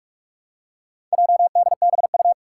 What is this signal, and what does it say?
Morse code, heard as a keyed tone, about 44 wpm, 700 Hz. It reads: JDBU
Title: Four Challenging Letter-Number Combinations 45wpm